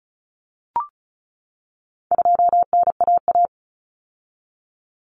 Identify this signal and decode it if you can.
Morse code, heard as a keyed tone, about 34 wpm, 700 Hz. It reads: 2NAA
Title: Common Cut Numbers - 4-Digit 35wpm